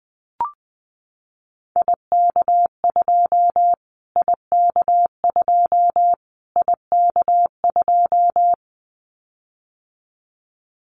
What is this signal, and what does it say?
Morse code, heard as a keyed tone, about 19 wpm, 700 Hz. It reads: IK2 IK2 IK2
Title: Callsign Prefixes for Top 25 DXCC - Repeated 3x - 20wpm